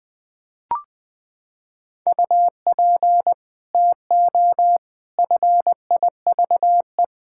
Morse code, heard as a keyed tone, about 20 wpm, 700 Hz. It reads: UP TO FIVE
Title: Sets of 6 Words - 20wpm